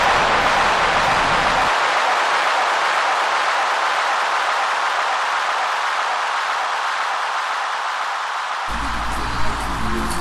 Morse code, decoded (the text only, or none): none